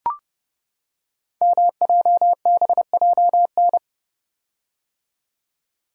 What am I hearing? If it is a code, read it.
Morse code, heard as a keyed tone, about 30 wpm, 700 Hz. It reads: MJBJN
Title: Code Groups - 5 Letters - 30wpm